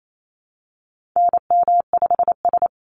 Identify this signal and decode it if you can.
Morse code, heard as a keyed tone, about 28 wpm, 700 Hz. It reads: NM5S
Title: NA SOTA Activator Callsigns 28wpm